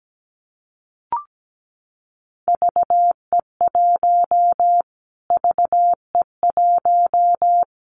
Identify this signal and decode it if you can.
Morse code, heard as a keyed tone, about 17 wpm, 700 Hz. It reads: VE1 VE1
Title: Callsign Prefixes for Top 25 DXCC - Repeated 3x - 17wpm